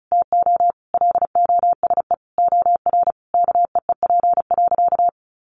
Morse code, heard as a keyed tone, about 34 wpm, 700 Hz. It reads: TO LOSE OR KEEP.
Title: Sentences from Top 1100 Words 35wpm